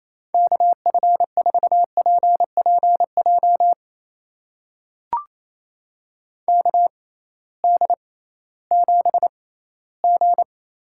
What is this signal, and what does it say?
Morse code, heard as a keyed tone, about 28 wpm, spaced as for 16 wpm, 700 Hz. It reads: KF4PPJ KD7G
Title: USA Call Signs - 6x Character Spacing - 28wpm